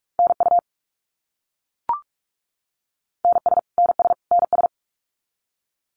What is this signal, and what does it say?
Morse code, heard as a keyed tone, about 44 wpm, 700 Hz. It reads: NU NS NS NS
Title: Canadian Provinces and Territories - Repeated 3x - 45wpm